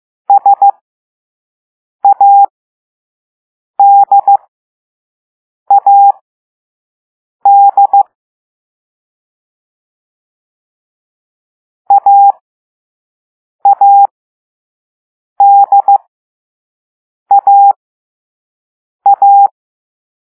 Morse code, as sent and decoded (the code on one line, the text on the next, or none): ... .- -.. .- -.. / .- .- -.. .- .-
SADAD AADAA